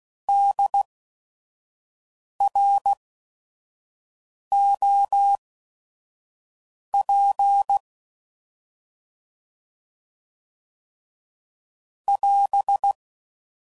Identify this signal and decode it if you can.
Morse code, heard as a keyed tone, about 16 wpm, spaced as for 5 wpm, 800 Hz. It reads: DROP <AS>
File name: Prefixes / By First Letter / Ds